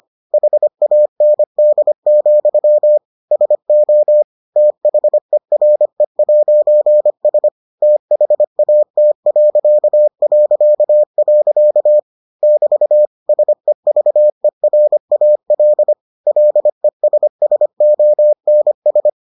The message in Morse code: HAND, SO THERE'S THAT... <BT> SEVERAL LESSONS

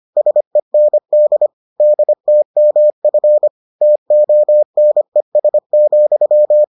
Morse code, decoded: SEND DTMF TONES,

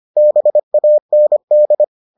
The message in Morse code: BAND

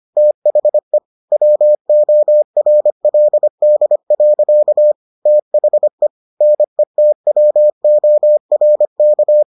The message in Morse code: THE WORLD. THE NETWORK

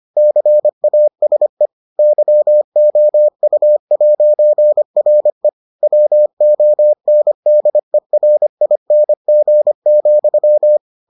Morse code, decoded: CASE YOU'RE WONDERING,

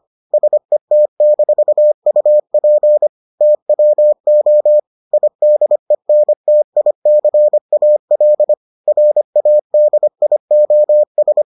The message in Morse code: SET-UP TWO IDENTICAL RADIOS

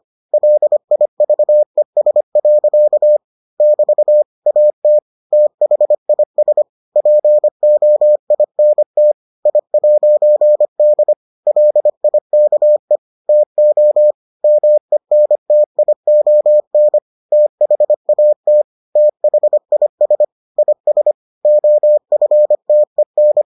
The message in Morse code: LIVES. <BT> AT THIS POINT I'D LIKE TO MENTION THAT THIS IS OFTEN

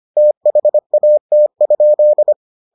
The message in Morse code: THAT?

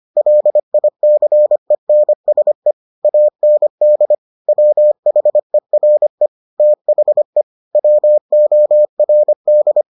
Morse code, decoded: LICENSE AND WHERE THE WORD

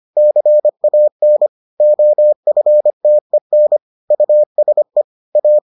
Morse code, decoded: CAN OFTEN USE A

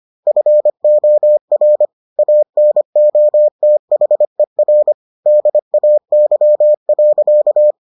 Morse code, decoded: FOR ANOTHER DAY.

